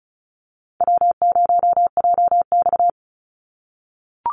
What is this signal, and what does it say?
Morse code, heard as a keyed tone, about 34 wpm, 700 Hz. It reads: W0JX
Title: Call Signs - 35wpm